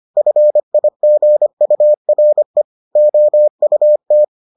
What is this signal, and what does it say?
Morse code, heard as a keyed tone, about 25 wpm, 600 Hz. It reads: FIGURE OUT